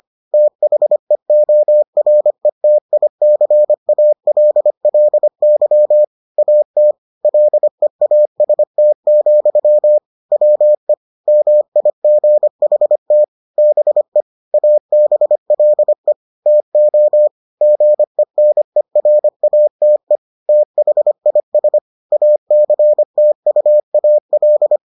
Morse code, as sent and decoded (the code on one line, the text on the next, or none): - .... --- .-. - .. -.-. .- .-.. .-.. -.-- / .- - / .-.. .- ... - --..-- / .-- . / -- .. --. .... - / -... / .- -... .-.. / - --- / --. -. .-. .- - . / - .... .. ... / .- -.-. - ..- .- .-..
THEORETICALLY AT LEAST, WE MIGHT BE ABLE TO GENERATE THIS ACTUAL